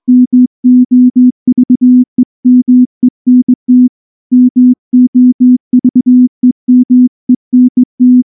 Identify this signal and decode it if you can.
Morse code, heard as a keyed tone, about 20 wpm, 250 Hz. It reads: MOVEMENT MOVEMENT
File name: morse / morse-training-20WPM-3